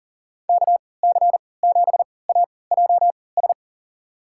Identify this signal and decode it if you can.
Morse code, heard as a keyed tone, about 38 wpm, spaced as for 22 wpm, 700 Hz. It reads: KC7AJS
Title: USA Call Signs - 6 Characters - 3x Character Spacing - 40wpm